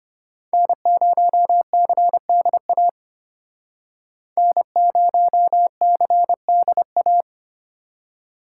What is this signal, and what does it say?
Morse code, heard as a keyed tone, about 32 wpm, 700 Hz. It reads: N0CDA N0CDA